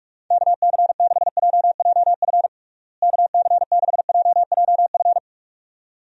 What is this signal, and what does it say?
Morse code, heard as a keyed tone, about 44 wpm, 700 Hz. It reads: KC6JJF KC6JJF
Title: USA Call Signs - 6 Characters - Repeated 2x - 45wpm